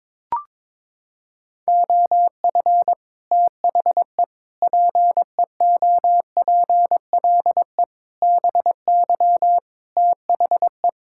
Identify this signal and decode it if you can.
Morse code, heard as a keyed tone, about 21 wpm, 700 Hz. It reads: OF THE PEOPLE BY THE
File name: Sets of 9 Words - 22wpm